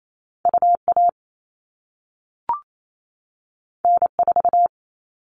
Morse code, heard as a keyed tone, about 28 wpm, 700 Hz. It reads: UA N4
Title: All Cut Numbers - 2-Digit 28wpm